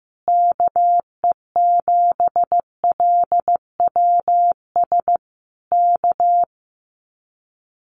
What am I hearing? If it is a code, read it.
Morse code, heard as a keyed tone, about 14 wpm, 700 Hz. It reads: KE7LWS K
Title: Common QSO Phrases Encore 15wpm